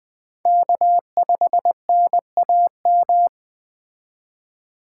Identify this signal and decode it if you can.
Morse code, heard as a keyed tone, about 20 wpm, 700 Hz. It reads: K5NAM
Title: Speed Racing - USA 1x3 Call Signs 17wpm